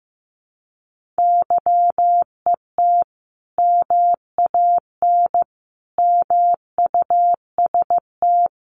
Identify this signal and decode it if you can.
Morse code, heard as a keyed tone, about 15 wpm, 700 Hz. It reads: YET MAN MUST
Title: Sentences from Top 1500 Words 15wpm